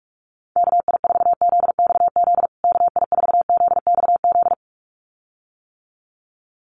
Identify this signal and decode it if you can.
Morse code, heard as a keyed tone, about 44 wpm, 700 Hz. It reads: KI4ZXZ KI4ZXZ
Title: USA Call Signs Repeated 2x 45wpm